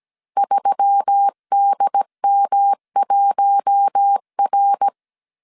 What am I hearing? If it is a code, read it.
Morse code, heard as a keyed tone, about 17 wpm, 800 Hz. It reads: 3DM1R
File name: CW / DET / 1 6 1500C S A 16wpm 2635